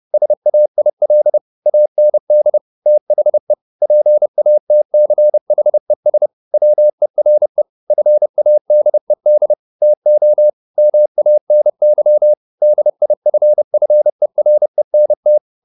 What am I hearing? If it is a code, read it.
Morse code, heard as a keyed tone, about 30 wpm, 600 Hz. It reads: SAIL AND THE PATCHES WERE FADED TO MANY DIFFERENT
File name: the old man and the sea / the old man and the sea 1 30wpm